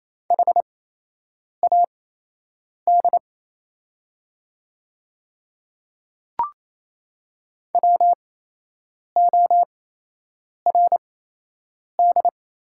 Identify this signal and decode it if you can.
Morse code, heard as a keyed tone, about 28 wpm, spaced as for 8 wpm, 700 Hz. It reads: HAD WORD
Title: 8x Character Spacing - Top 100 Words - 28wpm